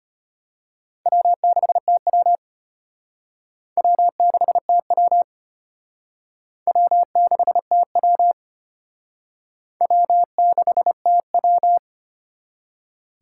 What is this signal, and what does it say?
Morse code, heard as a keyed tone, about 33 wpm, 700 Hz. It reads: W6TW W6TW W6TW W6TW